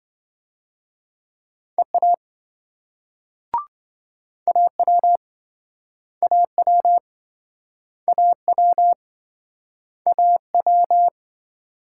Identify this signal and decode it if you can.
Morse code, heard as a keyed tone, about 27 wpm, 700 Hz. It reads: EA AW AW AW AW